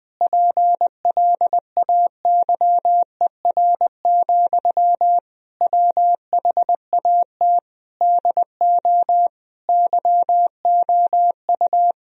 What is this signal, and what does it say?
Morse code, heard as a keyed tone, about 21 wpm, 700 Hz. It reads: PLAYER, WHAT DO YOU
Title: Sentences from Top 1000 Words - Review Part 4 - 20wpm